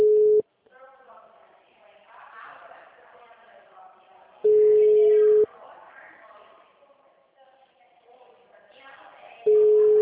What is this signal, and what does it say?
Ringback tone (Europe, 425 Hz), with speech.
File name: out- -2003- - - .70113